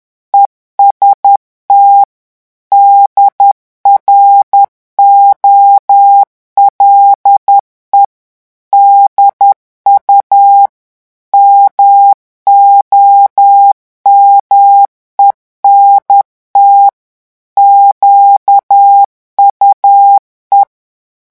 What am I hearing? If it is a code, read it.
Morse code, heard as a keyed tone, about 10 wpm, 800 Hz. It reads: EST DROLE DU MOMENT QUE
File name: morse mp3 / revision francais 10 mots minute